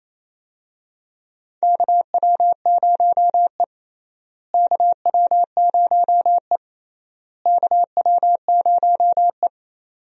Morse code, heard as a keyed tone, about 28 wpm, 700 Hz. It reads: KW0E KW0E KW0E
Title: USA Call Signs - 4 Characters - Repeated 3x - 28wpm